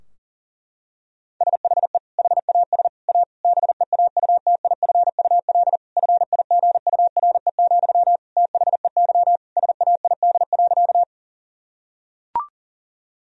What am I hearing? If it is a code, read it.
Morse code, heard as a keyed tone, about 40 wpm, 700 Hz. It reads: SHE HAS A BEAUTIFUL FIGURE, THEY SAID.
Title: Sentences from Top 1000 Words - Review Part 2 - 40wpm